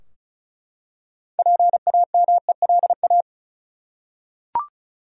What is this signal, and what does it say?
Morse code, heard as a keyed tone, about 36 wpm, 700 Hz. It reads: PAMELA
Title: USA Common Names - 35wpm